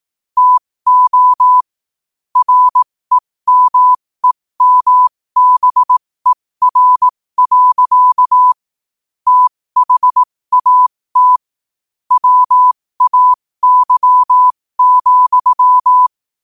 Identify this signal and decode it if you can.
Morse code, heard as a keyed tone, about 18 wpm, spaced as for 15 wpm, 1000 Hz. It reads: TO REMEMBER. THAT WAY,
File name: qotdfiles / 15wpm